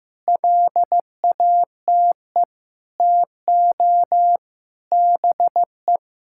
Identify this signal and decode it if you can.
Morse code, heard as a keyed tone, about 15 wpm, 700 Hz. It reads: LATE TO BE
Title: Sentences from Top 1600 Words 15wpm